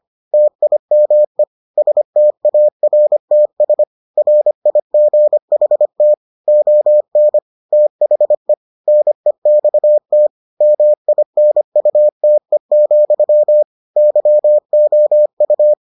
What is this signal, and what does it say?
Morse code, heard as a keyed tone, about 25 wpm, 600 Hz. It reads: TIME STARTS RIGHT ON THE NEXT MINUTE, YOU